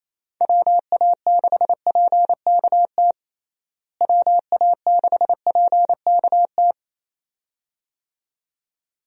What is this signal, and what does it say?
Morse code, heard as a keyed tone, about 28 wpm, 700 Hz. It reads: WA6PKT WA6PKT
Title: USA Call Signs - 6 Characters - Repeated 2x - 28wpm